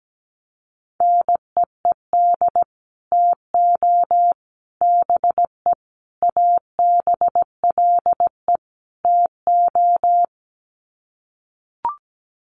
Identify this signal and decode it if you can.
Morse code, heard as a keyed tone, about 17 wpm, 700 Hz. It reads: NEED TO BE ABLE TO
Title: Sets of 5 Words Encore 17wpm